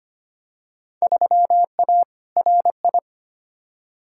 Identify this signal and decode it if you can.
Morse code, heard as a keyed tone, about 25 wpm, 700 Hz. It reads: 3A RI
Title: Field Day Exchanges - 25wpm